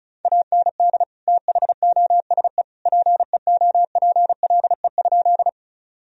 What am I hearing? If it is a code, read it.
Morse code, heard as a keyed tone, about 34 wpm, 700 Hz. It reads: AND THOSE PEOPLE?
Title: Sentences from Top 1000 Words - Review Part 1 - 35wpm